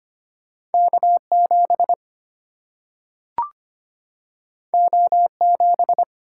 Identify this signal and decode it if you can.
Morse code, heard as a keyed tone, about 25 wpm, 700 Hz. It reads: K7 O7
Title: Single-Letter Single-Number 25wpm